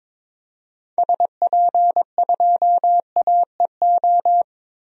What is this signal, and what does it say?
Morse code, heard as a keyed tone, about 22 wpm, 700 Hz. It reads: SP2AEO
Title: DXCC-100 Call Signs - 6 Characters - 22wpm